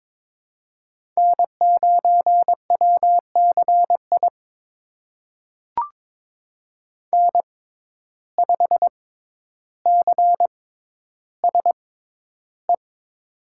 Morse code, heard as a keyed tone, about 22 wpm, spaced as for 11 wpm, 700 Hz. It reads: N9WCI N5CSE
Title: USA Call Signs - 5 Characters - 6x Character Spacing - 22wpm